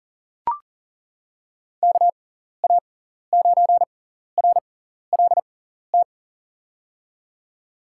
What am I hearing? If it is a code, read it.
Morse code, heard as a keyed tone, about 40 wpm, spaced as for 14 wpm, 700 Hz. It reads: KA9RLT
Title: USA Call Signs - 6 Characters - 6x Character Spacing - 40wpm